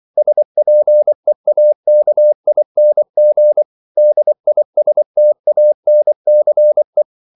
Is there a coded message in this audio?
Yes: Morse code from 0.2 to 7.0 s, heard as a keyed tone, about 24 wpm, 600 Hz. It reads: SPEAKING DISTANCE